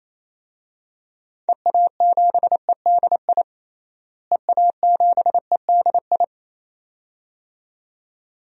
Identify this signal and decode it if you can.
Morse code, heard as a keyed tone, about 28 wpm, 700 Hz. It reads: EA7EDI EA7EDI
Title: DXCC-100 Call Signs - 6 Characters - Repeated 2x - 28wpm